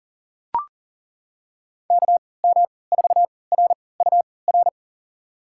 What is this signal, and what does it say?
Morse code, heard as a keyed tone, about 40 wpm, spaced as for 23 wpm, 700 Hz. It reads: KM4RUR
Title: USA Call Signs - 6 Characters - 3x Character Spacing - 40wpm